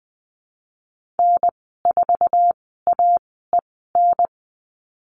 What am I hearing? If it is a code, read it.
Morse code, heard as a keyed tone, about 20 wpm, spaced as for 14 wpm, 700 Hz. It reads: N4AEN
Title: 062 Lesson 11g - Letter h - Call Signs 20wpm